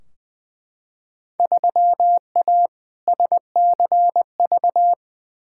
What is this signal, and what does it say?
Morse code, heard as a keyed tone, about 20 wpm, 700 Hz. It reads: 3A SCV
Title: Field Day Exchanges - 20wpm